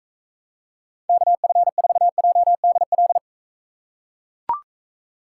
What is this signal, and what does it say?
Morse code, heard as a keyed tone, about 43 wpm, 700 Hz. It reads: KF4JDL